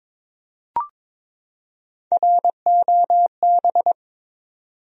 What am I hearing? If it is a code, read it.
Morse code, heard as a keyed tone, about 21 wpm, 700 Hz. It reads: ROB
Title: USA Common Names - Max 3 Letters - 22wpm